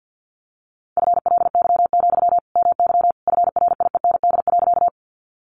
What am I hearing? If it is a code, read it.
Morse code, heard as a keyed tone, about 50 wpm, 700 Hz. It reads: FLY, MY FRIEND.
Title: Sentences from Top 500 Words 50wpm